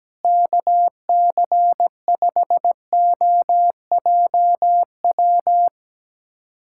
Morse code, heard as a keyed tone, about 17 wpm, 700 Hz. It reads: KC5OJW
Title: USA Callsign DE Callsign 17wpm